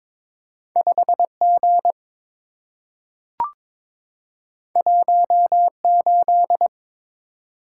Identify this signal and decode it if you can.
Morse code, heard as a keyed tone, about 22 wpm, 690 Hz. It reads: 5G 18
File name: practice / 22wpm / Two Letter-Number Combinations - 22wpm